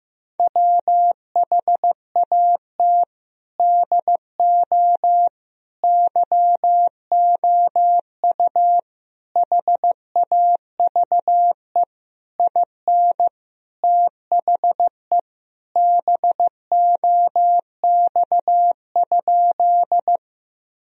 Morse code, heard as a keyed tone, about 16 wpm, 700 Hz. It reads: WHAT DO YOU HAVE IN THE BOX?
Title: Sentences from Top 1000 Words - Review Part 2 - 15wpm